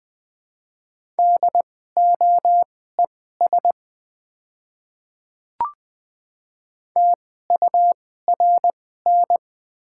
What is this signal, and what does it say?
Morse code, heard as a keyed tone, about 20 wpm, spaced as for 15 wpm, 700 Hz. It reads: DOES TURN
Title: 2x Character Spacing - Top 200 Words - 20wpm